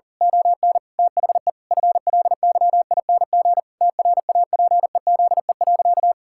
Morse code, heard as a keyed tone, about 37 wpm, 700 Hz. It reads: ON THE FLYING TRAPEZE.
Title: Pangram Challenge v2 - 40wpm